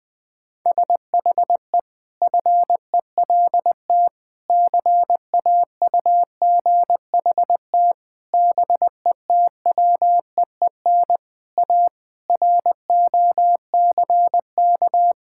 Morse code, heard as a keyed tone, about 21 wpm, 700 Hz. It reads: SHE FELT CAUGHT BETWEEN A ROCK
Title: Sentences with 8-Word Phrases - 20wpm